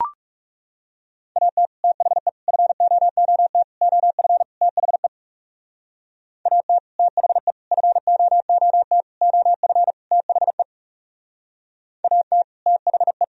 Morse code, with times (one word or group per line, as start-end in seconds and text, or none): 1.4-1.7 AT
1.8-2.3 THE
2.5-3.6 FOOT
3.8-4.4 OF
4.6-5.1 THE
6.5-6.8 AT
7.0-7.5 THE
7.7-9.0 FOOT
9.2-9.9 OF
10.1-10.6 THE
12.0-12.4 AT
12.7-13.2 THE